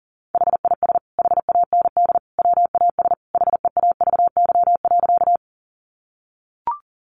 Morse code, heard as a keyed tone, about 40 wpm, 700 Hz. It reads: HIS HAND WAS HEAVY.
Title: Sentences from Top 1100 Words 40wpm